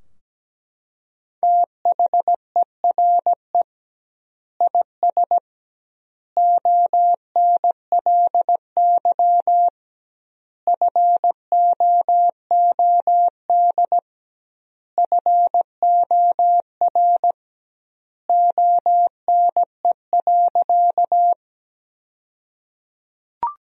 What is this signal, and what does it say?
Morse code, heard as a keyed tone, about 17 wpm, 700 Hz. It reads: THERE IS ONLY FOOD FOR ONE.
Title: 2x Word-Spacing - Sentences from Top 300 Words - 17wpm